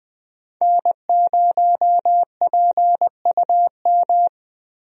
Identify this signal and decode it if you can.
Morse code, heard as a keyed tone, about 20 wpm, 700 Hz. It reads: N0PUM
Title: USA Call Signs - 5 Characters - Repeated 2x - 20wpm